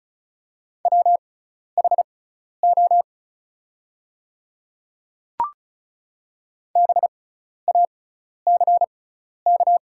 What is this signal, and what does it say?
Morse code, heard as a keyed tone, about 36 wpm, spaced as for 12 wpm, 700 Hz. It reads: WHO BACK
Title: 6x Character Spacing - Top 300 Words - 35wpm